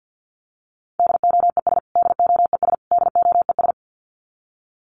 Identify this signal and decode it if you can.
Morse code, heard as a keyed tone, about 50 wpm, 700 Hz. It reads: DOES DOES DOES
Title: Top 500 Words Repeated 50wpm